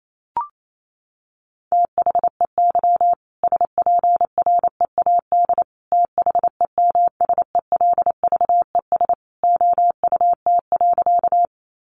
Morse code, heard as a keyed tone, about 28 wpm, 700 Hz. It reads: THEY SPREAD THEMSELVES OUT.
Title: Sentences from Top 1200 Words 28wpm